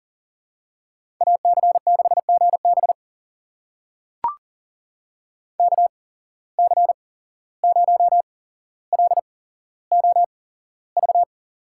Morse code, heard as a keyed tone, about 40 wpm, spaced as for 16 wpm, 700 Hz. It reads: AC6GB KC0LOV
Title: USA Call Signs - 8x Character Spacing - 40wpm